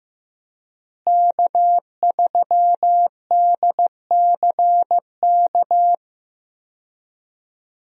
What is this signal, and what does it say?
Morse code, heard as a keyed tone, about 15 wpm, 700 Hz. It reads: K3DCK
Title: USA Call Signs - 5 Characters - Repeated 2x - 15wpm